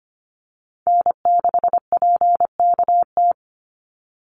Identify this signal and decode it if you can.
Morse code, heard as a keyed tone, about 25 wpm, 700 Hz. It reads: N6PKT
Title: NA SOTA Chaser Callsigns 25wpm